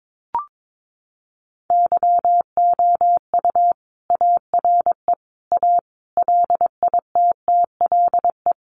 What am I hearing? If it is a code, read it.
Morse code, heard as a keyed tone, about 22 wpm, 700 Hz. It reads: YOU ARE A LITTLE
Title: Sentences from Top 1000 Words 22wpm